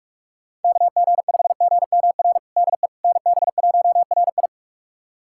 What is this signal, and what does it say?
Morse code, heard as a keyed tone, about 41 wpm, 700 Hz. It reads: KG5GMR DE NB1RI
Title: USA Callsign DE Callsign 45wpm